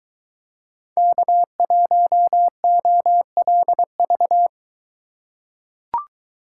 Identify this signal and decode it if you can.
Morse code, heard as a keyed tone, about 23 wpm, 700 Hz. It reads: K1OLV